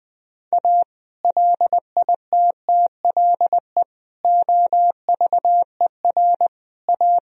Morse code, heard as a keyed tone, about 20 wpm, 700 Hz. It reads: A LITTLE OVER A
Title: Sentences from Top 1000 Words - Review Part 1 - 20wpm